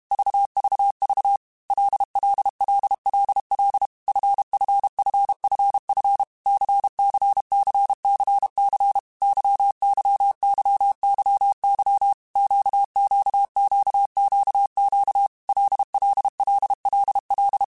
Morse code, heard as a keyed tone, about 32 wpm, 800 Hz. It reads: VVV LLLLL FFFFF CCCCC YYYYY QQQQQ LLLLL
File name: Groups@30 / Groups@30 mp3s / LFCYQ@30